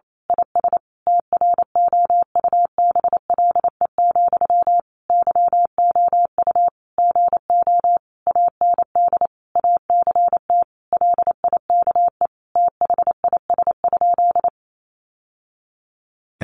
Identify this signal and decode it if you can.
Morse code, heard as a keyed tone, about 28 wpm, 700 Hz. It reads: IS TROUBLE, YOU GO AND ACT LIKE THIS?
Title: Sentences from Top 1200 Words 28wpm